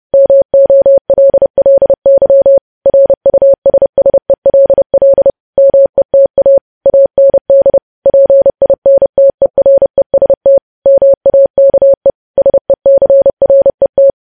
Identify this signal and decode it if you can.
Morse code, heard as a keyed tone, about 30 wpm, 550 Hz. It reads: MOLLY RUSSELL META AND PINTEREST MAKE SECRET
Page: news headlines in morse code at 30 wpm